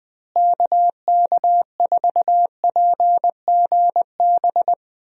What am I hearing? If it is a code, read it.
Morse code, heard as a keyed tone, about 20 wpm, 700 Hz. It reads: KK4PGB
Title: USA Call Signs - 6 Characters - Repeated 2x - 20wpm